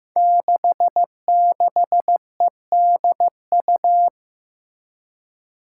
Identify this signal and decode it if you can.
Morse code, heard as a keyed tone, about 15 wpm, 700 Hz. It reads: 66EDU
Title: Five Challenging Letter-Number Combinations 15wpm